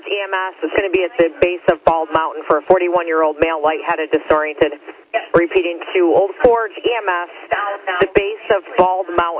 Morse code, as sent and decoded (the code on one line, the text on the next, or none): none